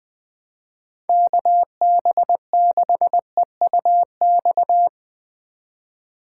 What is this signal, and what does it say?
Morse code, heard as a keyed tone, about 20 wpm, 700 Hz. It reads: KB6EUX
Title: Speed Racing - USA 2x3 Call Signs 17wpm